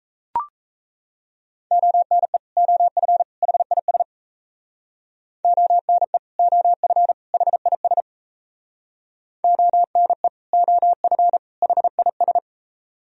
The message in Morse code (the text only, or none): ONE OF HIS ONE OF HIS ONE OF HIS